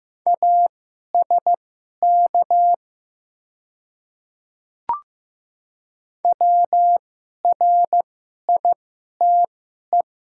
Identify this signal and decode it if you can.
Morse code, heard as a keyed tone, about 15 wpm, spaced as for 11 wpm, 700 Hz. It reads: ASK WRITE
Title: 2x Character Spacing - Top 300 Words - 15wpm